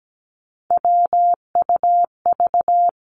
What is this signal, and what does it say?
Morse code, heard as a keyed tone, about 17 wpm, 700 Hz. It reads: WUV
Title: Three Challenging Letter-Number Combinations 17wpm